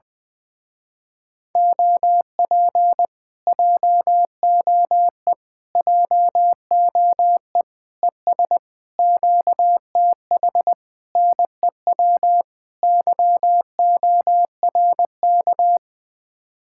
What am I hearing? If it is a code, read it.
Morse code, heard as a keyed tone, about 21 wpm, 700 Hz. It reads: OP JOE JOE ES QTH NEW YORK
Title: Speed-Racing Common QSO Phrases - 20wpm